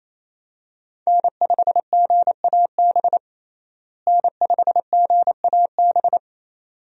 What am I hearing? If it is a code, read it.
Morse code, heard as a keyed tone, about 28 wpm, 700 Hz. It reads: N5GAB N5GAB
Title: USA Call Signs - 5 Characters - Repeated 2x - 28wpm